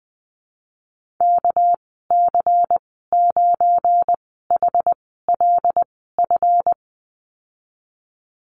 Morse code, heard as a keyed tone, about 20 wpm, spaced as for 15 wpm, 700 Hz. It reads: KC9HLF
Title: 200 Lesson 32g - Letter k - Call Signs 20wpm